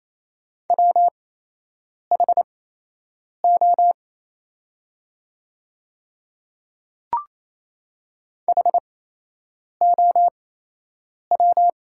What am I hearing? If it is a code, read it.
Morse code, heard as a keyed tone, about 28 wpm, spaced as for 8 wpm, 690 Hz. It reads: WHO HOW
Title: 8x Character Spacing - Top 100 Words - 28wpm